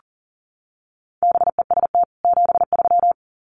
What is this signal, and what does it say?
Morse code, heard as a keyed tone, about 40 wpm, 700 Hz. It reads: BEST 73
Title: Common QSO Phrases 40wpm